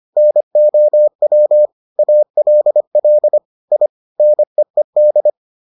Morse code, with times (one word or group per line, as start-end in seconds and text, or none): 0.2-1.7 NOW
2.0-3.4 ALL
3.7-3.9 I
4.2-5.3 NEED